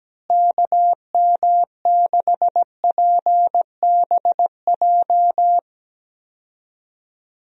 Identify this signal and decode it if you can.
Morse code, heard as a keyed tone, about 17 wpm, 700 Hz. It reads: KM6PBJ
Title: USA Call Signs - 6 Characters - Repeated 2x - 17wpm